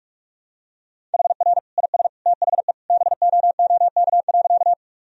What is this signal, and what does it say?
Morse code, heard as a keyed tone, about 41 wpm, 700 Hz. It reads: HR IS THE BOOK.
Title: Sentences from Top 300 Words and Abbreviations 45wpm